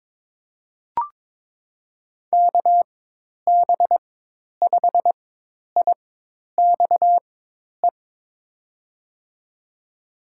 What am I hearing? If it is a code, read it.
Morse code, heard as a keyed tone, about 23 wpm, spaced as for 10 wpm, 700 Hz. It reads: KB5IXE
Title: USA Call Signs - 4x Character Spacing - 22wpm